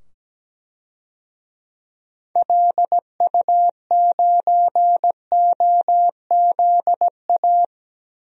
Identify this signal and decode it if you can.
Morse code, heard as a keyed tone, about 17 wpm, 700 Hz. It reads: LU9OZA
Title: DXCC-100 Call Signs - 6 Characters - 17wpm